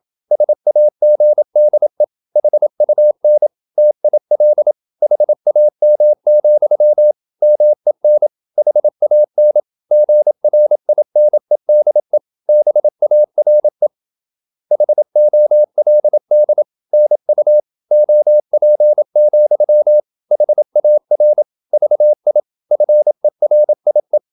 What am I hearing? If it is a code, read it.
Morse code, heard as a keyed tone, about 27 wpm, 600 Hz. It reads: SAGDE HUN TIL HAM, MEN HAN GRINEDE BARE HOLD NU OP, HAR VI FERIE
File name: Ferie QRQ 27wpm